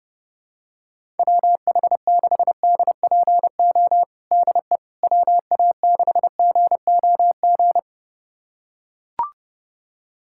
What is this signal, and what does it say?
Morse code, heard as a keyed tone, about 30 wpm, 700 Hz. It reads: WH6DPO DE WA6GOG
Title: USA Callsign DE Callsign 30wpm